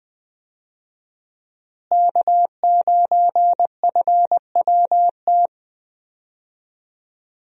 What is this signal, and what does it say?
Morse code, heard as a keyed tone, about 21 wpm, 700 Hz. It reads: K9FWT
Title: USA Call Signs - 5 Characters - Repeated 3x - 20wpm